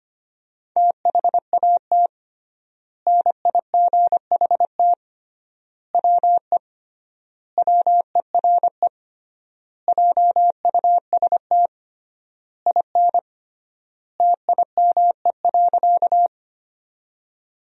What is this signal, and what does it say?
Morse code, heard as a keyed tone, about 25 wpm, 700 Hz. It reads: THAT NIGHT WE WERE JUST IN TIME.
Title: 3x Word-Spacing - Sentences from Top 300 Words - 25wpm